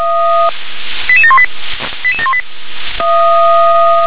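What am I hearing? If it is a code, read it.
Five-tone selective call (ZVEI), heard as quick runs of five tones, tones of about 70 ms. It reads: 88718 88718
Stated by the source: ZVEI five-tone selective call 88559 received on an SDR.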